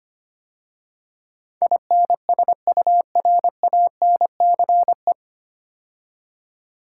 Morse code, heard as a keyed tone, about 25 wpm, 690 Hz. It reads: INSURANCE
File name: practice / 25wpm / 9 Letter Words - First 3 Letters Given - 25wpm